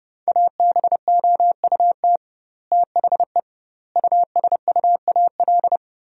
Morse code, heard as a keyed tone, about 32 wpm, 700 Hz. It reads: ABOUT THE USUAL
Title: 2x Word-Spacing - Sentences from Top 400 Words - 30wpm